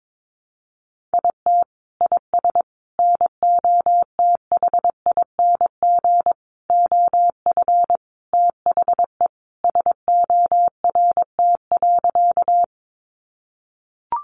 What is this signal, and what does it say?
Morse code, heard as a keyed tone, about 22 wpm, 700 Hz. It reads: IT IS NOTHING OF THE SORT.
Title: Sentences from Top 1300 Words 22wpm